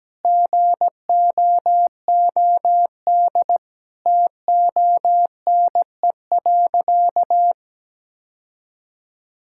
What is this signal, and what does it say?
Morse code, heard as a keyed tone, about 17 wpm, 700 Hz. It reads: GOOD TONE.
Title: Sentences from Top 1600 Words 17wpm